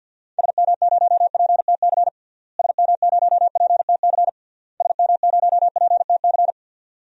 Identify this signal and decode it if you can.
Morse code, heard as a keyed tone, about 52 wpm, 700 Hz. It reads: SM0PTC SM0PTC SM0PTC
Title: DXCC-100 Call Signs - 6 Characters - Repeated 3x - 50wpm